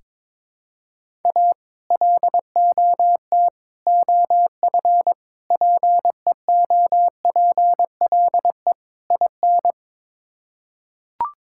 Morse code, heard as a keyed tone, about 22 wpm, 700 Hz. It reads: A LOT OF PEOPLE IN
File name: practice / 22wpm / Build A Phrase - 5-Word Phrases - 22wpm